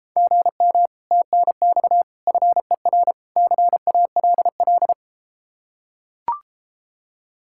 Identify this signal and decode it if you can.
Morse code, heard as a keyed tone, about 33 wpm, 700 Hz. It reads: GM TNX FER CALL